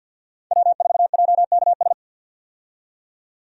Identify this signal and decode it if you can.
Morse code, heard as a keyed tone, about 52 wpm, 700 Hz. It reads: W4JKS
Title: USA Call Signs - 5 Characters - 50wpm